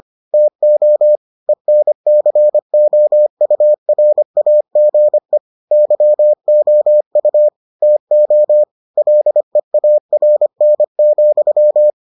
Morse code, heard as a keyed tone, about 25 wpm, 600 Hz. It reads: TO ENCOURAGE YOU TO LEARN,